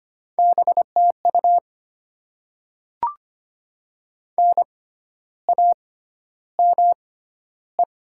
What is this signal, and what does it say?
Morse code, heard as a keyed tone, about 25 wpm, spaced as for 12 wpm, 690 Hz. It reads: BTU NAME
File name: Top 90 QSO Elements - 6x Character Spacing - 25wpm